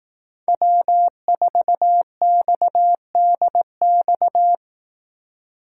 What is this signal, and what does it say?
Morse code, heard as a keyed tone, about 18 wpm, 700 Hz. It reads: W4XDX
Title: Speed Racing - USA 1x3 Call Signs 15wpm